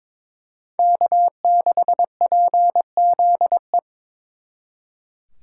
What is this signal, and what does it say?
Morse code, heard as a keyed tone, about 26 wpm, 700 Hz. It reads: K6PZE
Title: Speed Racing - USA 1x3 Call Signs 22wpm